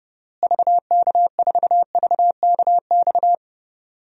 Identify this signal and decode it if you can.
Morse code, heard as a keyed tone, about 30 wpm, 700 Hz. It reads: VK4VKX
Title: DXCC-100 Call Signs - 6 Characters - Repeated 2x - 30wpm